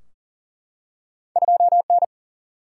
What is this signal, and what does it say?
Morse code, heard as a keyed tone, about 42 wpm, 700 Hz. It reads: 2N
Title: Single-Letter Single-Number 40wpm